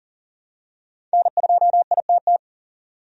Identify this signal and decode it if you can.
Morse code, heard as a keyed tone, about 40 wpm, 700 Hz. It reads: N2ITT
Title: USA Call Signs - 5 Characters - 40wpm